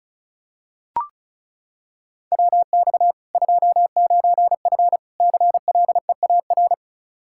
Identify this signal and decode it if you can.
Morse code, heard as a keyed tone, about 34 wpm, 700 Hz. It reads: WX 29F CLEAR
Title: Practice QSO - Exchange 2 - Info - 35wpm